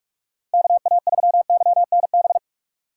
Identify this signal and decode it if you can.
Morse code, heard as a keyed tone, about 42 wpm, 700 Hz. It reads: KA3YNB